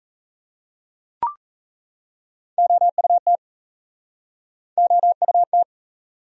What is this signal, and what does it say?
Morse code, heard as a keyed tone, about 37 wpm, 700 Hz. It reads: OUT OUT